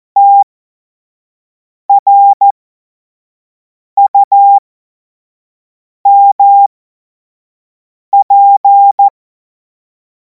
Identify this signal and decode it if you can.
Morse code, heard as a keyed tone, about 14 wpm, spaced as for 5 wpm, 800 Hz. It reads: TRUMP
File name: Morsekurs / Witze / Witz 25. August 2026 25 F70